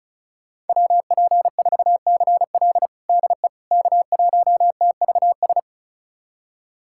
Morse code, heard as a keyed tone, about 34 wpm, 700 Hz. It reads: WP4CL DE K1TVS
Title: USA Callsign DE Callsign - 5 Characters - 35wpm